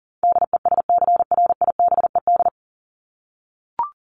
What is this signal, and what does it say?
Morse code, heard as a keyed tone, about 40 wpm, 700 Hz. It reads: DESCRIBED
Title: ED Suffix - Top 100 Words with Base Words 40wpm